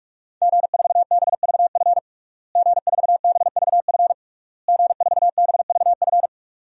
Morse code, heard as a keyed tone, about 46 wpm, 700 Hz. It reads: G4BVF G4BVF G4BVF
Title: DXCC-100 Call Signs - 5 Characters - Repeated 3x - 45wpm